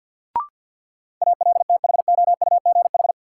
Morse code, heard as a keyed tone, about 50 wpm, 700 Hz. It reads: ALTHOUGH
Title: Sentences from Top 600 Words - New Word Drill - 50wpm